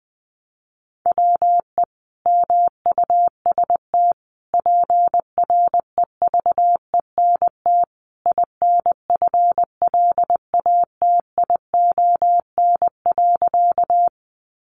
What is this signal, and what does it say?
Morse code, heard as a keyed tone, about 20 wpm, 700 Hz. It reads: WE MUST PREVENT INFLATION.
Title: Sentences from Top 1100 Words 20wpm